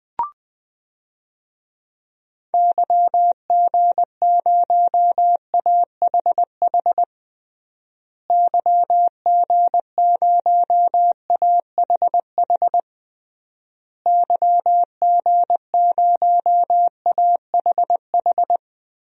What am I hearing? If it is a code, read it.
Morse code, heard as a keyed tone, about 21 wpm, 700 Hz. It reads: YG0AHH YG0AHH YG0AHH
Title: DXCC-100 Call Signs - 6 Characters - Repeated 3x - 20wpm